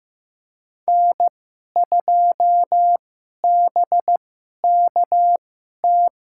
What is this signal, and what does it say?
Morse code, heard as a keyed tone, about 15 wpm, spaced as for 11 wpm, 700 Hz. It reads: N2BKT
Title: USA Call Signs - 5 Characters - 2x Character Spacing - 15wpm